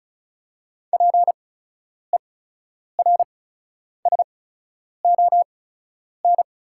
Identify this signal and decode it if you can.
Morse code, heard as a keyed tone, about 36 wpm, spaced as for 10 wpm, 700 Hz. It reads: PERSON
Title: 8x Character Spacing - Top 500 Words - 35wpm